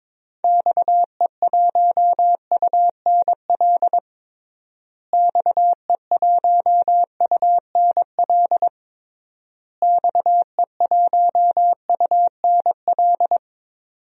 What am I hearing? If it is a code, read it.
Morse code, heard as a keyed tone, about 22 wpm, 700 Hz. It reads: XE1UNL XE1UNL XE1UNL
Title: DXCC-100 Call Signs - 6 Characters - Repeated 3x - 22wpm